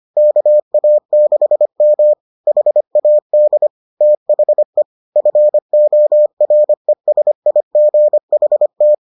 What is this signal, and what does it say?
Morse code, heard as a keyed tone, about 25 wpm, 600 Hz. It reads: KA6M HAD THE FORESIGHT